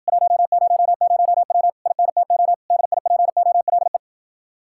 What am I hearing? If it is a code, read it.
Morse code, heard as a keyed tone, about 50 wpm, 700 Hz. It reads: 100W INTO DIPOLE
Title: Speed-Racing Common QSO Phrases - 40wpm